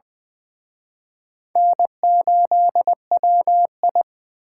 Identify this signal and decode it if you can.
Morse code, heard as a keyed tone, about 24 wpm, 700 Hz. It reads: N8WI
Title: Speed Racing - USA 1x2 Call Signs 20wpm